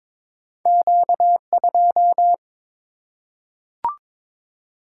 Morse code, heard as a keyed tone, about 22 wpm, 690 Hz. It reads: Q2
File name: practice / 22wpm / Two Letter-Number Combinations - 22wpm